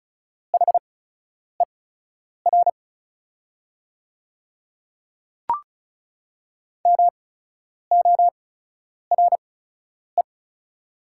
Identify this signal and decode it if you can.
Morse code, heard as a keyed tone, about 34 wpm, spaced as for 10 wpm, 700 Hz. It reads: HER MORE
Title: 8x Character Spacing - Top 100 Words - 35wpm